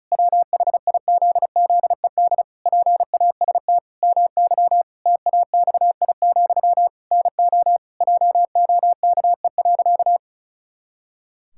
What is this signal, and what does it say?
Morse code, heard as a keyed tone, about 36 wpm, 700 Hz. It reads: WHIZZED PAST MY TAXI, NO JOKE.
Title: Pangram Challenge v2 - 35wpm